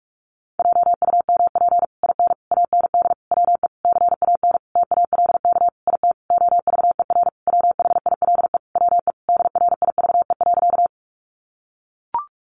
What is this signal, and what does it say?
Morse code, heard as a keyed tone, about 44 wpm, 700 Hz. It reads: JUMP IN AND WE CAN TALK IT OVER WHILE WE DRIVE.
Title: Sentences from Top 1300 Words 45wpm